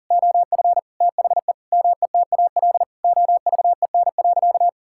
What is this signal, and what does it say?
Morse code, heard as a keyed tone, about 37 wpm, 700 Hz. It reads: OF THE METAL OVEN.
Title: Sentences from Top 1000 Words - Review Part 6 - 40wpm